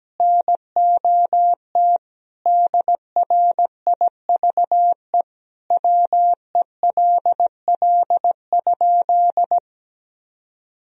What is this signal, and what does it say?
Morse code, heard as a keyed tone, about 17 wpm, 700 Hz. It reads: NOT DRIVE WELL?
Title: Sentences from Top 1000 Words - Review Part 2 - 17wpm